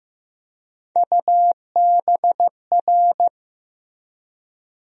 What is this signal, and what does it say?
Morse code, heard as a keyed tone, about 15 wpm, 700 Hz. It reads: UBR
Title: USA Call Sign Postfixes - 3 Letters 15wpm